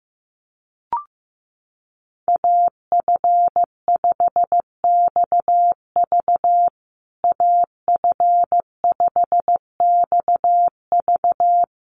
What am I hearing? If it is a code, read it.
Morse code, heard as a keyed tone, about 15 wpm, 700 Hz. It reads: AF5XV AF5XV
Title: USA Call Signs Repeated 2x 15wpm